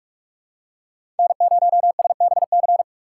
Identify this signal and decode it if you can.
Morse code, heard as a keyed tone, about 41 wpm, 700 Hz. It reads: N0SBC
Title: USA Call Signs - 5 Characters - 4x Character Spacing - 45wpm